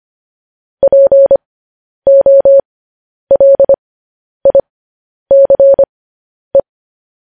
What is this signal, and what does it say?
Morse code, heard as a keyed tone, about 25 wpm, spaced as for 10 wpm, 550 Hz. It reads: POLICE
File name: podcast / News headlines-1x-25x10